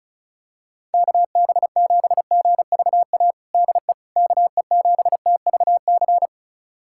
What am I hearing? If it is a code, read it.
Morse code, heard as a keyed tone, about 36 wpm, 700 Hz. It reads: KB7GVA DE KE7TVC
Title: USA Callsign DE Callsign - 6 Characters - 35wpm